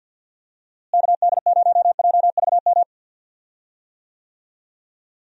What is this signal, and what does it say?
Morse code, heard as a keyed tone, about 48 wpm, 700 Hz. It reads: KD0JVM
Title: USA Call Signs - 6 Characters - Repeated 2x - 50wpm